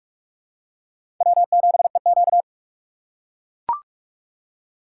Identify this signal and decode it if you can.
Morse code, heard as a keyed tone, about 42 wpm, 700 Hz. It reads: W7EQ
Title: USA Call Signs - 4 Characters - 3x Character Spacing - 45wpm